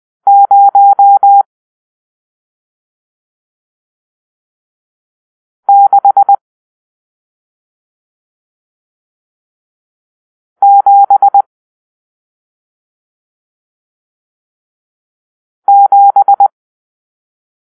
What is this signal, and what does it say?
Morse code, heard as a keyed tone, about 20 wpm, 800 Hz. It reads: 0 6 7 7
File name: arrl / Grp-7 Drill-1